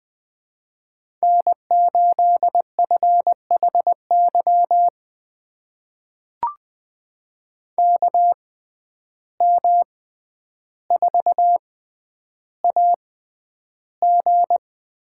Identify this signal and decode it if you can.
Morse code, heard as a keyed tone, about 21 wpm, spaced as for 10 wpm, 700 Hz. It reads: N8FHY KM4AG
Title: USA Call Signs - 5 Characters - 6x Character Spacing - 20wpm